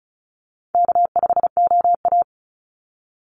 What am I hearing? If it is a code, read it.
Morse code, heard as a keyed tone, about 34 wpm, 700 Hz. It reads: K5OA
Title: Call Signs Repeated 2x - 35wpm